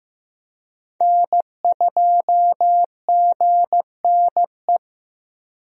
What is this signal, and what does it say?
Morse code, heard as a keyed tone, about 15 wpm, 700 Hz. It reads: N2GNE
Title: USA Call Signs - 5 Characters - Repeated 3x - 15wpm